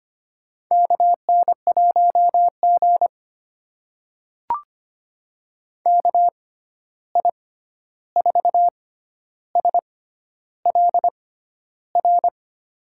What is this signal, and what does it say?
Morse code, heard as a keyed tone, about 25 wpm, spaced as for 11 wpm, 700 Hz. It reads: KN1G KI4SLR
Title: USA Call Signs - 6x Character Spacing - 25wpm